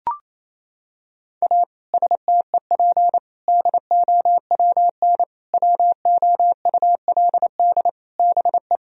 Morse code, heard as a keyed tone, about 28 wpm, 700 Hz. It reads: A STEP DOWN WOULD BE
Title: Sentences from Top 1000 Words - Review Part 4 - 28wpm